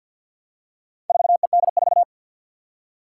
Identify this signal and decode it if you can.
Morse code, heard as a keyed tone, about 53 wpm, 700 Hz. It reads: 4ED4
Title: Four Challenging Letter-Number Combinations 50wpm